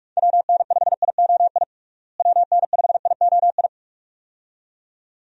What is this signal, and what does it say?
Morse code, heard as a keyed tone, about 46 wpm, 700 Hz. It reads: WN5IOI WN5IOI
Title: USA Call Signs - 6 Characters - Repeated 2x - 45wpm